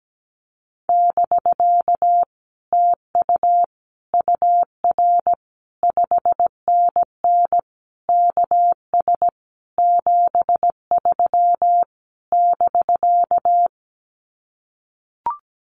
Morse code, heard as a keyed tone, about 17 wpm, 700 Hz. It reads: <BK> TU UR 5NN KS 73 <BK>
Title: Common QSO Phrases Encore 17wpm